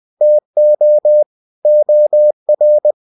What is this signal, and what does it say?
Morse code, heard as a keyed tone, about 20 wpm, 600 Hz. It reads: TO OR
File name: cw-books / winnie-the-pooh / winnie-20wpm-page-0116